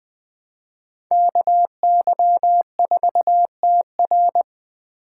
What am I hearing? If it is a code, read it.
Morse code, heard as a keyed tone, about 21 wpm, 700 Hz. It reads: KY4TR
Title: Speed Racing - USA 2x2 Call Signs 17wpm